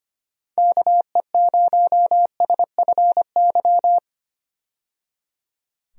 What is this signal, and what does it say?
Morse code, heard as a keyed tone, about 25 wpm, 700 Hz. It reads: KE0SFY
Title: USA Call Signs - 6 Characters - 25wpm